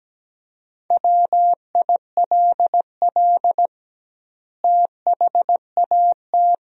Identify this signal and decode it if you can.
Morse code, heard as a keyed tone, about 17 wpm, 700 Hz. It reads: WILL THAT
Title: 2x Word-Spacing - Sentences from Top 100 Words - 17wpm